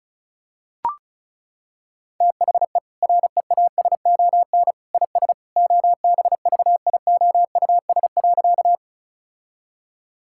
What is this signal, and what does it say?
Morse code, heard as a keyed tone, about 34 wpm, 700 Hz. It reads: THE REASON IS OBVIOUS.
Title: Sentences from Top 1600 Words 35wpm